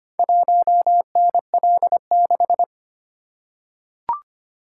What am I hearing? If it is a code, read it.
Morse code, heard as a keyed tone, about 25 wpm, 690 Hz. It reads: 1NL6
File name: practice / 25wpm / Four Letter-Number Combinations - 25wpm